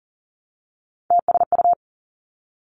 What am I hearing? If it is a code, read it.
Morse code, heard as a keyed tone, about 41 wpm, 700 Hz. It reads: TSU
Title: Three Challenging Letter-Number Combinations 40wpm